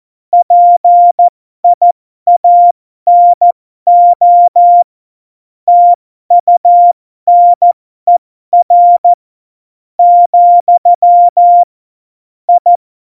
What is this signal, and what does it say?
Morse code, heard as a keyed tone, about 14 wpm, spaced as for 11 wpm, 700 Hz. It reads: PIANO TUNER, I
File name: Morsekurs / Witze